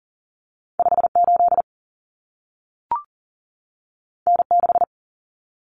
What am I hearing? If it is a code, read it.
Morse code, heard as a keyed tone, about 40 wpm, 700 Hz. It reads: 5 8 N 6
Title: Common Cut Numbers - 2-Digit 40wpm